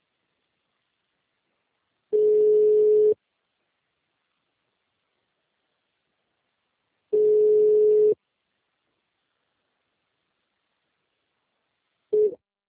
Ringback tone (Europe, 425 Hz).
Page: Nova Iguacu - RJ